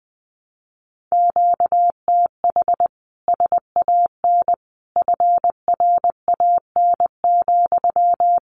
Morse code, heard as a keyed tone, about 20 wpm, 700 Hz. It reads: QTH SAN FRAN,
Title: Common QSO Phrases Encore 20wpm